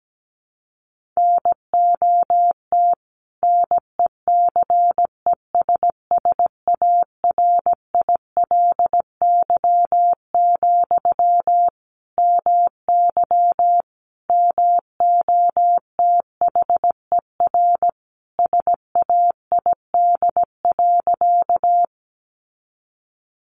Morse code, heard as a keyed tone, about 17 wpm, 700 Hz. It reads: NOT NECESSARILY, MY MOTHER SAID.
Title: Sentences from Top 1400 Words 17wpm